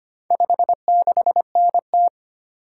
Morse code, heard as a keyed tone, about 25 wpm, 700 Hz. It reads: 56NT
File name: practice / 25wpm / Four Letter-Number Combinations - 25wpm